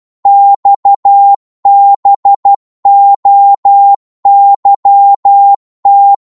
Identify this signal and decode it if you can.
Morse code, heard as a keyed tone, about 12 wpm, 800 Hz. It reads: XBOYT